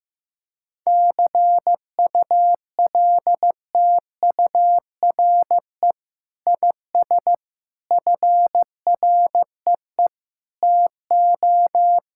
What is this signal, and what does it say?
Morse code, heard as a keyed tone, about 15 wpm, 700 Hz. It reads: CULTURE IS FREE TO
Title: Sentences from Top 1000 Words - Review Part 6 - 15wpm